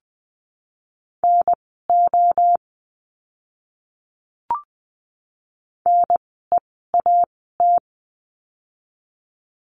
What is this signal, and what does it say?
Morse code, heard as a keyed tone, about 20 wpm, spaced as for 7 wpm, 700 Hz. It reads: NONE AT
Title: 024 Lesson 5f - Letter o - Words 20wpm